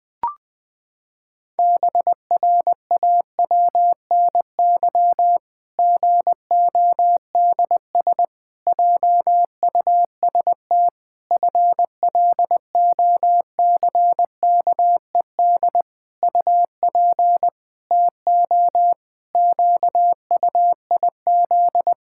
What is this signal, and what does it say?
Morse code, heard as a keyed tone, about 20 wpm, 700 Hz. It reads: BRAWNY GODS JUST FLOCKED UP TO QUIZ
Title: Pangram Challenge v2 - 20wpm